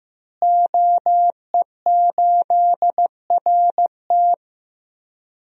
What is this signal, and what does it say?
Morse code, heard as a keyed tone, about 15 wpm, 700 Hz. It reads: OE8RT
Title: DXCC-100 Call Signs - 5 Characters - Repeated 3x - 15wpm